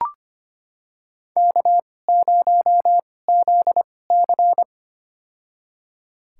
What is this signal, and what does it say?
Morse code, heard as a keyed tone, about 25 wpm, spaced as for 18 wpm, 700 Hz. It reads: K0ZC
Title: USA Call Signs - 4 Characters - 2x Character Spacing - 25wpm